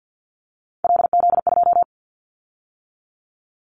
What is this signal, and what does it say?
Morse code, heard as a keyed tone, about 50 wpm, 700 Hz. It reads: LZ2
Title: Callsign Prefixes for Top 25 DXCC - 50wpm